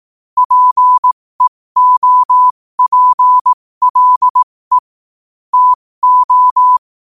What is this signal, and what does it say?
Morse code, heard as a keyed tone, about 18 wpm, spaced as for 15 wpm, 1000 Hz. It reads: PEOPLE TO
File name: qotdfiles / 15wpm